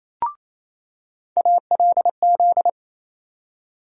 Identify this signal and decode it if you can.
Morse code, heard as a keyed tone, about 28 wpm, 700 Hz. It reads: ALZ
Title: USA Call Sign Postfixes - 3 Letters 28wpm